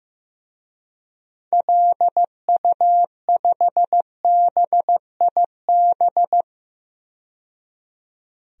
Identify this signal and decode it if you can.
Morse code, heard as a keyed tone, about 15 wpm, 700 Hz. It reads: LU5BIB
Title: DXCC-100 Call Signs - 6 Characters - 15wpm